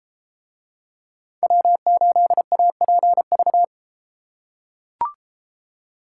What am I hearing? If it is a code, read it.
Morse code, heard as a keyed tone, about 31 wpm, 700 Hz. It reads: W8APV